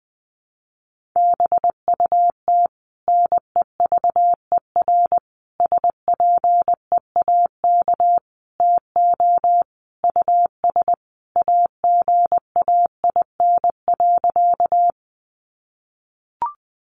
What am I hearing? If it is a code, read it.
Morse code, heard as a keyed tone, about 20 wpm, 700 Hz. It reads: BUT NEVER SPEAK TO US AGAIN.
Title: Sentences from Top 1100 Words 20wpm